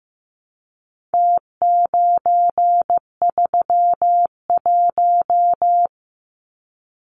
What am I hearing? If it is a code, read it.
Morse code, heard as a keyed tone, about 15 wpm, 700 Hz. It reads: T931
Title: Numbers - Contest Style - 4-Digit 15wpm